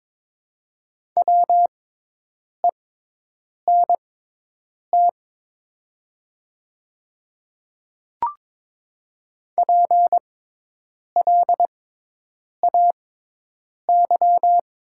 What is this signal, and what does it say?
Morse code, heard as a keyed tone, about 22 wpm, spaced as for 8 wpm, 700 Hz. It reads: WENT PLAY
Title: 6x Character Spacing - Top 200 Words - 22wpm